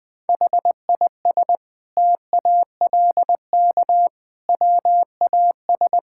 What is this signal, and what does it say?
Morse code, heard as a keyed tone, about 21 wpm, 700 Hz. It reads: HIS TALK WAS
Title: Sentences from Top 1000 Words - Review Part 6 - 20wpm